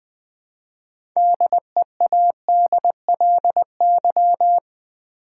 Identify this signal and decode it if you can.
Morse code, heard as a keyed tone, about 20 wpm, 700 Hz. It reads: DEADLY
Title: 6 Letter Words - First 3 Letters Given - 20wpm